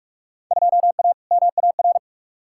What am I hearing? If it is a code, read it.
Morse code, heard as a keyed tone, about 47 wpm, 700 Hz. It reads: 2A MAR
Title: Field Day Exchanges - 45wpm